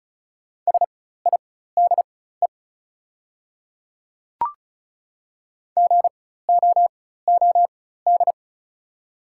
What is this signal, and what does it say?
Morse code, heard as a keyed tone, about 36 wpm, spaced as for 16 wpm, 700 Hz. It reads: SIDE GOOD
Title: 4x Character Spacing - Top 200 Words - 35wpm